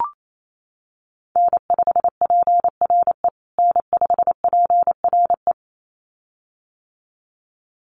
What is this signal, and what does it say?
Morse code, heard as a keyed tone, about 28 wpm, 700 Hz. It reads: N5PRE N5PRE
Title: USA Call Signs Repeated 2x 28wpm